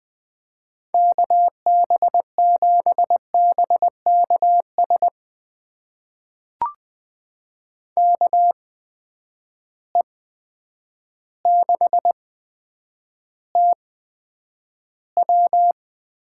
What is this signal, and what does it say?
Morse code, heard as a keyed tone, about 20 wpm, spaced as for 9 wpm, 700 Hz. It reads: KB7BKS KE6TW
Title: USA Call Signs - 8x Character Spacing - 20wpm